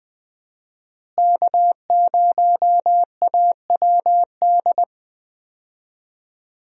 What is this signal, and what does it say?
Morse code, heard as a keyed tone, about 20 wpm, 700 Hz. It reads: K0AWD
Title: USA Call Signs - 5 Characters - Repeated 2x - 20wpm